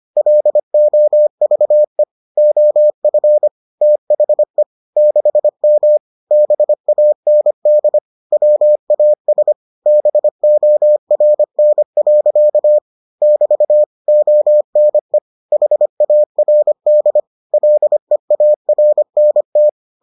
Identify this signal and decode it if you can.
Morse code, heard as a keyed tone, about 25 wpm, 600 Hz. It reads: LOVE OF THE 6M BAND WAS BORN. <BT> ONE HARD LEARNT